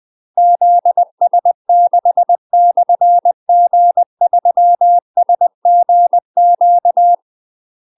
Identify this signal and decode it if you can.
Morse code, heard as a keyed tone, about 20 wpm, 700 Hz. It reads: ZS6/G3SGQ
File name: UK MASTER QSOs TRAINING / 20WPM / UK MASTER QSOs 0029